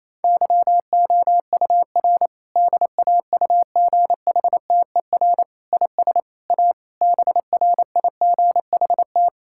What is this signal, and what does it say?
Morse code, heard as a keyed tone, about 28 wpm, 700 Hz. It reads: YOUR DAUGHTER IS A BRIGHT
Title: Sentences from Top 1600 Words 28wpm